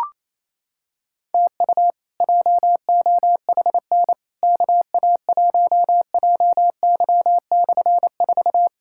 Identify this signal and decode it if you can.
Morse code, heard as a keyed tone, about 28 wpm, 700 Hz. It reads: TU JOHN KA1JY/4
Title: K1USN SST QSO Practice - 28wpm